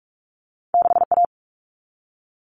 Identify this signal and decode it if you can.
Morse code, heard as a keyed tone, about 46 wpm, 700 Hz. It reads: 6A
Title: Common Cut Numbers - 2-Digit 45wpm